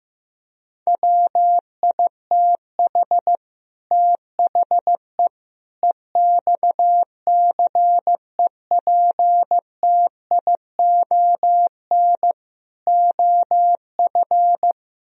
Morse code, heard as a keyed tone, about 15 wpm, 700 Hz. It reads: WITH THE EXCEPTION OF
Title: Speed-Racing - Sets of 4 Words 15wpm